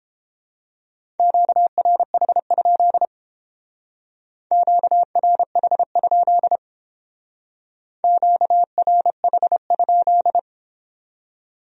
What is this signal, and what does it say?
Morse code, heard as a keyed tone, about 31 wpm, 700 Hz. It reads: QRH? QRH? QRH?